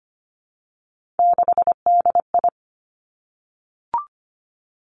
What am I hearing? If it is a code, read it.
Morse code, heard as a keyed tone, about 25 wpm, 700 Hz. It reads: 6DI
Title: Three Challenging Letter-Number Combinations 25wpm